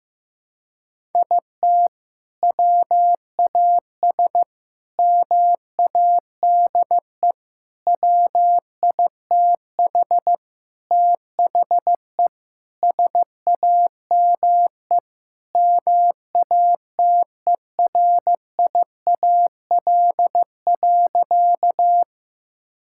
Morse code, heard as a keyed tone, about 15 wpm, 700 Hz. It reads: IT WAS MADE WITH THE SAME MATERIAL.
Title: Sentences from Top 1000 Words - Review Part 5 - 15wpm